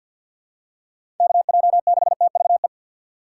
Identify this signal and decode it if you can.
Morse code, heard as a keyed tone, about 52 wpm, 700 Hz. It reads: KJ6TVE
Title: USA Call Signs - 6 Characters - 4x Character Spacing - 50wpm